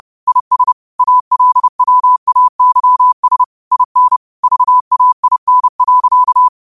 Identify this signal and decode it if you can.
Morse code, heard as a keyed tone, about 30 wpm, 1000 Hz. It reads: IS ALWAYS IN VAIN.
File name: qotdfiles / 30wpm